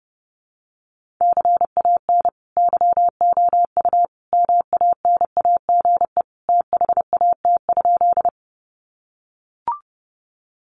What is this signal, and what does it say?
Morse code, heard as a keyed tone, about 30 wpm, 700 Hz. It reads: CAN YOU MANAGE THAT?
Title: Sentences from Top 1400 Words 30wpm